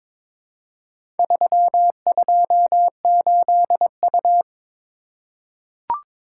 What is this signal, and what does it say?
Morse code, heard as a keyed tone, about 22 wpm, 690 Hz. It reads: 328U
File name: practice / 22wpm / Four Letter-Number Combinations - 22wpm